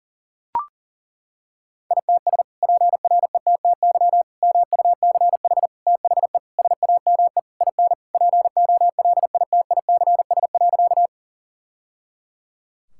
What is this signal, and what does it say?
Morse code, heard as a keyed tone, about 40 wpm, 700 Hz. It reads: ITS PRETTY MUCH THE SAME IN POLITICS.
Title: Sentences from Top 1000 Words - Review Part 6 - 40wpm